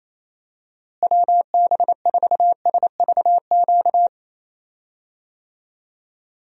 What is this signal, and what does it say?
Morse code, heard as a keyed tone, about 27 wpm, 700 Hz. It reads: WB4SVQ
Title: USA Call Signs - 6 Characters - Repeated 3x - 28wpm